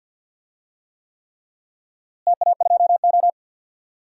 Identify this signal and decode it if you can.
Morse code, heard as a keyed tone, about 48 wpm, 700 Hz. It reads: TA2O
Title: DXCC-100 Call Signs - 4 Characters - Repeated 3x - 50wpm